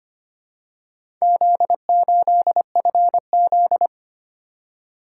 Morse code, heard as a keyed tone, about 25 wpm, 700 Hz. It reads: Z8FZ
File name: practice / 25wpm / Four Letter-Number Combinations - 25wpm